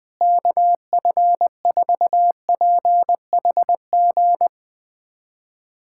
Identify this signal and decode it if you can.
Morse code, heard as a keyed tone, about 20 wpm, 700 Hz. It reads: KF4PHG
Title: USA Callsign DE Callsign 20wpm